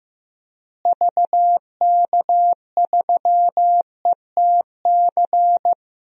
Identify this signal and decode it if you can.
Morse code, heard as a keyed tone, about 15 wpm, 700 Hz. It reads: VK3ETC
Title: DXCC-100 Call Signs - 6 Characters - Repeated 3x - 15wpm